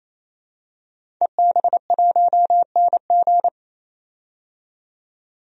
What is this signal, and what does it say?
Morse code, heard as a keyed tone, about 28 wpm, 700 Hz. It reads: EB1NG
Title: DXCC-100 Call Signs - 5 Characters - 28wpm